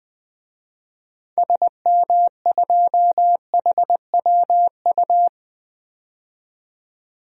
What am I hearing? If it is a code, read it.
Morse code, heard as a keyed tone, about 20 wpm, 700 Hz. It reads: SM2HWU
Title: DXCC-100 Call Signs - 6 Characters - 20wpm